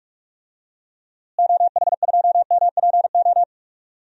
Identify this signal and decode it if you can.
Morse code, heard as a keyed tone, about 46 wpm, 700 Hz. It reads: OH2MPO
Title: DXCC-100 Call Signs - 6 Characters - Repeated 3x - 45wpm